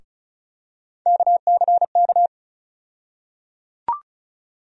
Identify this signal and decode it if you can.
Morse code, heard as a keyed tone, about 36 wpm, 700 Hz. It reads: KCK
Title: USA Call Sign Postfixes - 3 Letters 35wpm